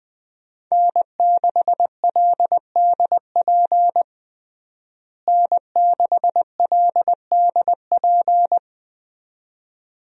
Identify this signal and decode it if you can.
Morse code, heard as a keyed tone, about 20 wpm, 700 Hz. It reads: N6LDP N6LDP
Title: USA Call Signs - 5 Characters - Repeated 2x - 20wpm